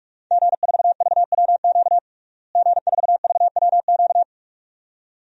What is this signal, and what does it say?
Morse code, heard as a keyed tone, about 46 wpm, 700 Hz. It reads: G4VWQ G4VWQ
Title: DXCC-100 Call Signs - 5 Characters - Repeated 2x - 45wpm